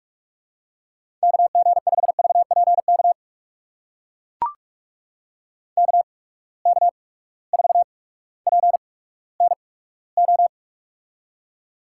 Morse code, heard as a keyed tone, about 44 wpm, spaced as for 19 wpm, 700 Hz. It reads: KG5VPK KK4PNO
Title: USA Call Signs - 6 Characters - 8x Character Spacing - 45wpm